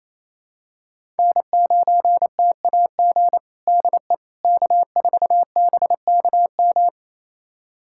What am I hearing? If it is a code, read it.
Morse code, heard as a keyed tone, about 28 wpm, 700 Hz. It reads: N9TAG DE K4BKM
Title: USA Callsign DE Callsign - 5 Characters - 28wpm